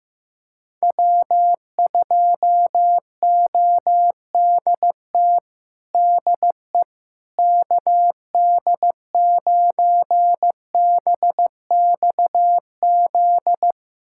Morse code, heard as a keyed tone, about 15 wpm, 700 Hz. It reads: W2ODT DE KD9BXZ
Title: USA Callsign DE Callsign 15wpm